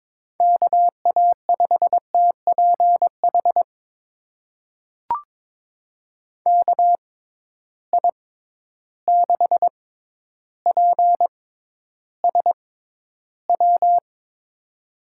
Morse code, heard as a keyed tone, about 22 wpm, spaced as for 11 wpm, 700 Hz. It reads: KA5TPH KI6PSW
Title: USA Call Signs - 6 Characters - 6x Character Spacing - 22wpm